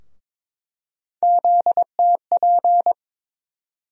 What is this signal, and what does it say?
Morse code, heard as a keyed tone, about 22 wpm, 700 Hz. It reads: ZTP
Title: USA Call Sign Postfixes - 3 Letters 22wpm